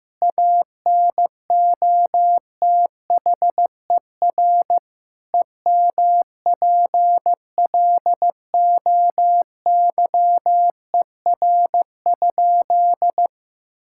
Morse code, heard as a keyed tone, about 16 wpm, 700 Hz. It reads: ANOTHER EMPLOYER?
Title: Sentences from Top 1600 Words 15wpm